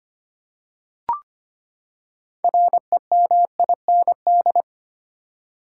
Morse code, heard as a keyed tone, about 25 wpm, 700 Hz. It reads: REMIND
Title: Six Letter Words - 25wpm